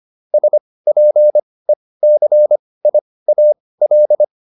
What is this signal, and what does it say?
Morse code, heard as a keyed tone, about 25 wpm, spaced as for 18 wpm, 600 Hz. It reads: SPECIAL